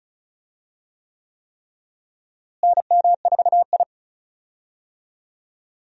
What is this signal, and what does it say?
Morse code, heard as a keyed tone, about 36 wpm, 700 Hz. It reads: NM4I
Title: USA Call Signs - 4 Characters - Repeated 3x - 35wpm